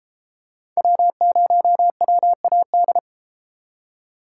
Morse code, heard as a keyed tone, about 33 wpm, 700 Hz. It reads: W0WAD